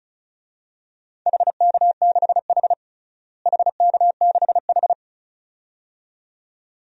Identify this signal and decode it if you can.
Morse code, heard as a keyed tone, about 34 wpm, 700 Hz. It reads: HK6H HK6H
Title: DXCC-100 Call Signs - 4 Characters - Repeated 2x - 35wpm